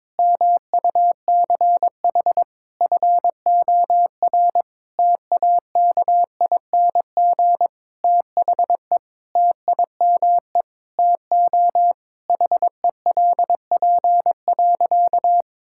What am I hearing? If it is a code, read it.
Morse code, heard as a keyed tone, about 22 wpm, 700 Hz. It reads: MUCH FOR TAKING THE TIME TO HELP.
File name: Sentences with 9-Word Phrases - 22wpm